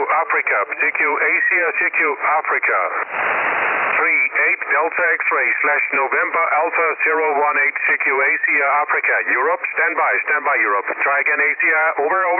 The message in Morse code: none